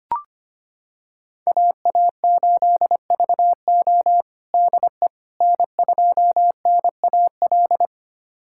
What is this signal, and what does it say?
Morse code, heard as a keyed tone, about 25 wpm, 700 Hz. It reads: AA8VO DE N2NAL
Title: USA Callsign DE Callsign - 5 Characters - 25wpm